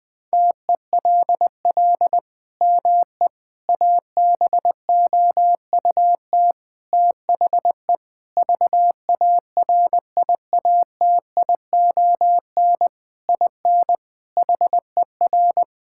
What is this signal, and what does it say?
Morse code, heard as a keyed tone, about 20 wpm, 700 Hz. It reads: TELL ME ABOUT THE VARIATION IN HER
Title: Sentences from Top 1600 Words 20wpm